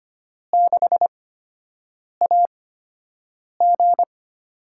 Morse code, heard as a keyed tone, about 25 wpm, spaced as for 7 wpm, 700 Hz. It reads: 6AG
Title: USA Call Signs - 8x Character Spacing - 25wpm